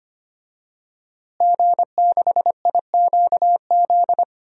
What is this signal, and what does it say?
Morse code, heard as a keyed tone, about 25 wpm, 700 Hz. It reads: G6IQZ
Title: DXCC-100 Call Signs - 5 Characters - Repeated 2x - 25wpm